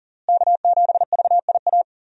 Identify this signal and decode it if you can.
Morse code, heard as a keyed tone, about 40 wpm, 700 Hz. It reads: K7VIA
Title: USA Call Signs - 5 Characters - Repeated 2x - 40wpm